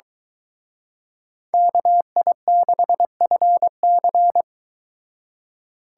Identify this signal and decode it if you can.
Morse code, heard as a keyed tone, about 23 wpm, 700 Hz. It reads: KI6FC